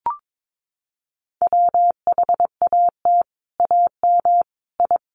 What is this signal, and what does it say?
Morse code, heard as a keyed tone, about 22 wpm, 700 Hz. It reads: WHAT AM I
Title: Sets of 5 Words Encore2 22wpm